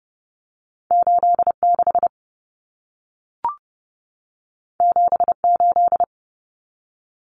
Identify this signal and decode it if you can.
Morse code, heard as a keyed tone, about 30 wpm, 700 Hz. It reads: 86 78
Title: Numbers - 2-Digit 30wpm